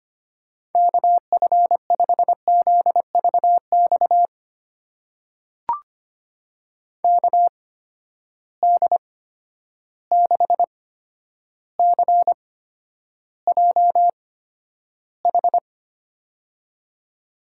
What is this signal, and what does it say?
Morse code, heard as a keyed tone, about 25 wpm, spaced as for 11 wpm, 700 Hz. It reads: KF5ZVX KD6CJH
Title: USA Call Signs - 6 Characters - 8x Character Spacing - 25wpm